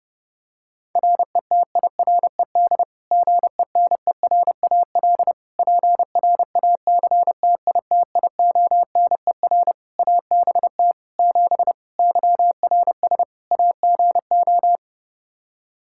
Morse code, heard as a keyed tone, about 32 wpm, 700 Hz. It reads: RETIRED GENERAL PRACTITIONER ABT 7 YRS AGO
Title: Practice QSO - Exchange 3 - More Info - 30wpm